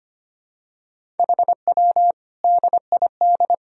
Morse code, heard as a keyed tone, about 25 wpm, 700 Hz. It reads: HW DID
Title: Sentences from Top 200 Words and Abbreviations 25wpm